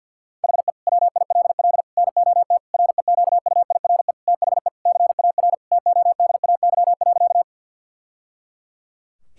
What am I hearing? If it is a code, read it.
Morse code, heard as a keyed tone, about 47 wpm, 700 Hz. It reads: HE WILL NOT REQUIRE THE CAR TODAY.
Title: Sentences from Top 1000 Words - Review Part 6 - 50wpm